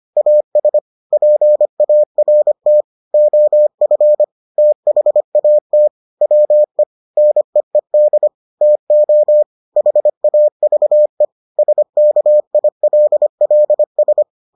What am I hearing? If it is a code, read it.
Morse code, heard as a keyed tone, about 25 wpm, 600 Hz. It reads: AS PART OF THAT WE NEED TO HAVE SKILLS